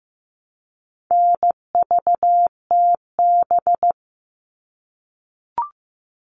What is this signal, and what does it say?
Morse code, heard as a keyed tone, about 15 wpm, 700 Hz. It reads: NVTB
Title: All Cut Numbers - 4-Digit 15wpm